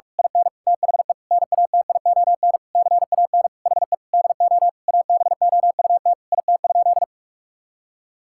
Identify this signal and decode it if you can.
Morse code, heard as a keyed tone, about 46 wpm, 700 Hz. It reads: IN THE NATION CAN HE DO ABOUT IT?
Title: Sentences from Top 1000 Words - Review Part 6 - 45wpm